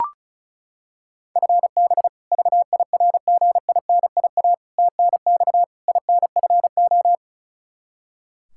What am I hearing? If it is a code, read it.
Morse code, heard as a keyed tone, about 36 wpm, 700 Hz. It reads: FB VIRGINIA TNX INFO
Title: K1USN SST QSO Practice 35wpm